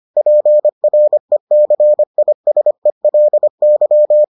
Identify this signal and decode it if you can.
Morse code, heard as a keyed tone, about 25 wpm, 600 Hz. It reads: PRECISELY